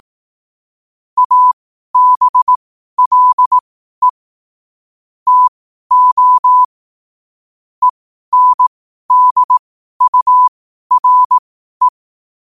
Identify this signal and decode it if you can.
Morse code, heard as a keyed tone, about 18 wpm, spaced as for 12 wpm, 1000 Hz. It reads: ABLE TO ENDURE
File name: qotdfiles / 12wpm